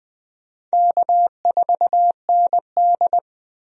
Morse code, heard as a keyed tone, about 20 wpm, 700 Hz. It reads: K4ND
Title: USA Call Signs - 4 Characters - 20wpm